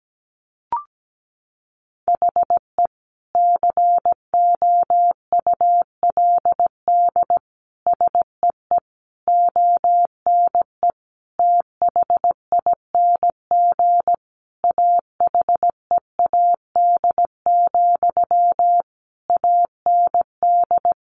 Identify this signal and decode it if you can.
Morse code, heard as a keyed tone, about 17 wpm, 700 Hz. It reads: HE COULD SEE ONE THING AHEAD, AND
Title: Sentences from Top 1100 Words 17wpm